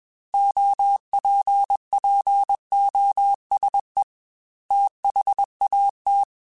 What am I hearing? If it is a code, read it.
Morse code, heard as a keyed tone, about 21 wpm, 800 Hz. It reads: OPPOSE THAT
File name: Technical Subjects / Technical Subjects-mp3s / XC @20